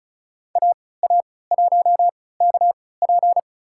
Morse code, heard as a keyed tone, about 36 wpm, spaced as for 20 wpm, 700 Hz. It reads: AA1KP
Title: USA Call Signs - 5 Characters - 3x Character Spacing - 35wpm